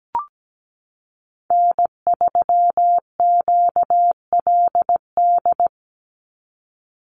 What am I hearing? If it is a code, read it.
Morse code, heard as a keyed tone, about 17 wpm, 700 Hz. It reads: N3QLD
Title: USA Call Signs 17wpm